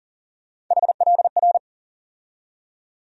Morse code, heard as a keyed tone, about 40 wpm, 700 Hz. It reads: HLR
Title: USA Call Sign Postfixes - 3 Letters 40wpm